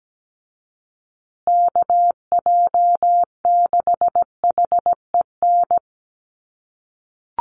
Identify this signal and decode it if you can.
Morse code, heard as a keyed tone, about 17 wpm, 700 Hz. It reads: KJ6HEN
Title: USA Call Signs Repeated 2x 17wpm